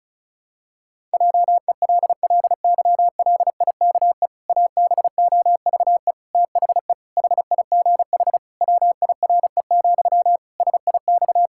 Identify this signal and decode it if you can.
Morse code, heard as a keyed tone, about 36 wpm, 700 Hz. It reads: JELLYLIKE ABOVE THE HIGH WIRE, SIX
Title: Pangram Challenge v2 - 35wpm